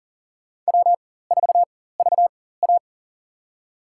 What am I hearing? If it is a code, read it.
Morse code, heard as a keyed tone, about 40 wpm, spaced as for 19 wpm, 700 Hz. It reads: W4VA
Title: USA Call Signs - 4 Characters - 4x Character Spacing - 40wpm